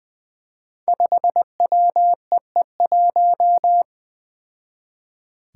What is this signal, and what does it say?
Morse code, heard as a keyed tone, about 20 wpm, 700 Hz. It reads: 5WEE1
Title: Five Challenging Letter-Number Combinations 20wpm